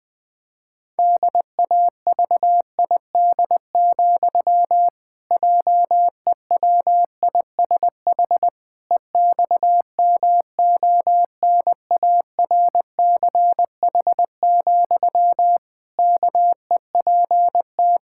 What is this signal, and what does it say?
Morse code, heard as a keyed tone, about 21 wpm, 700 Hz. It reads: DAVID, JEWISH EXMONARCH, KEPT
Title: Pangram Challenge v2 - 20wpm